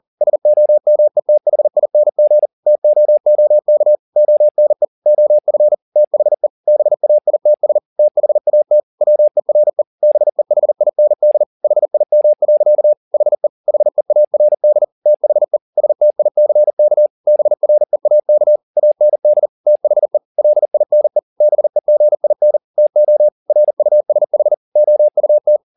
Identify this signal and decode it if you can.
Morse code, heard as a keyed tone, about 40 wpm, 600 Hz. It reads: SOMETHING TOOK ONE OF THE BAITS THAT WERE BEHIND HIM. HE HEARD THE STICK BREAK AND THE LINE BEGIN TO RUSH OUT